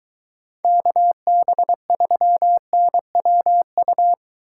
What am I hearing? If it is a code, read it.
Morse code, heard as a keyed tone, about 23 wpm, 700 Hz. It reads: KB3NWU